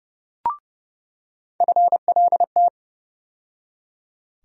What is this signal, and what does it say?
Morse code, heard as a keyed tone, about 29 wpm, 700 Hz. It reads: FLT
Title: USA Call Sign Postfixes - 3 Letters 30wpm